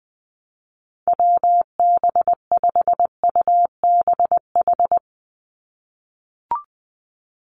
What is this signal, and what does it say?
Morse code, heard as a keyed tone, about 20 wpm, 700 Hz. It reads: WB5UBH
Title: USA Call Signs 20wpm